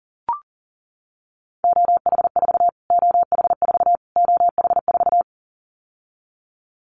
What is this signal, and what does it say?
Morse code, heard as a keyed tone, about 40 wpm, 700 Hz. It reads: OH4 OH4 OH4
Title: Callsign Prefixes for Top 25 DXCC - Repeated 3x - 40wpm